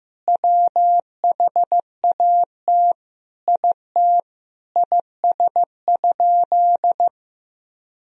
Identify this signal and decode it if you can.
Morse code, heard as a keyed tone, about 15 wpm, 700 Hz. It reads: WHAT IT IS?
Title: Sentences from Top 100 Words and Abbreviations 15wpm